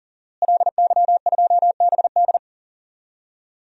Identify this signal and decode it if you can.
Morse code, heard as a keyed tone, about 40 wpm, 700 Hz. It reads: LY2BD
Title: DXCC-100 Call Signs - 5 Characters - Repeated 3x - 40wpm